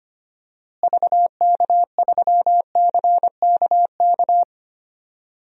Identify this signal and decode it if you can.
Morse code, heard as a keyed tone, about 25 wpm, 700 Hz. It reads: VK3CKK
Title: DXCC-100 Call Signs - 6 Characters - Repeated 3x - 25wpm